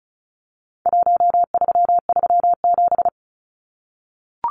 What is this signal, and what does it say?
Morse code, heard as a keyed tone, about 34 wpm, 700 Hz. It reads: 1337
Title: Numbers - 4-Digit 35wpm